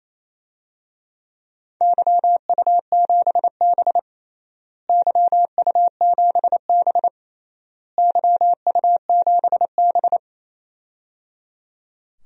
Morse code, heard as a keyed tone, about 28 wpm, 700 Hz. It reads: YU7B YU7B YU7B
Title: DXCC-100 Call Signs - 4 Characters - Repeated 3x - 28wpm